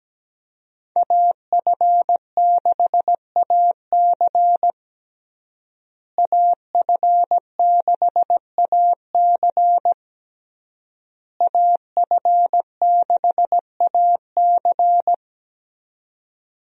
Morse code, heard as a keyed tone, about 17 wpm, 700 Hz. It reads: AF6AC AF6AC AF6AC
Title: USA Call Signs - 5 Characters - Repeated 3x - 17wpm